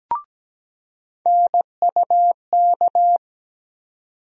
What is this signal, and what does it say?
Morse code, heard as a keyed tone, about 17 wpm, 700 Hz. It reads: NUK
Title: USA Call Sign Postfixes - 3 Letters 17wpm